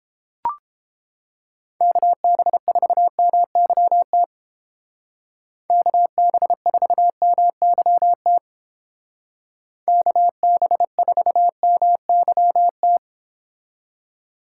Morse code, heard as a keyed tone, about 30 wpm, 700 Hz. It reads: KB4MYT KB4MYT KB4MYT